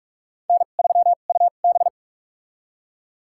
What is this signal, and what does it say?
Morse code, heard as a keyed tone, about 46 wpm, spaced as for 33 wpm, 700 Hz. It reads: N3UB
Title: USA Call Signs - 4 Characters - 2x Character Spacing - 45wpm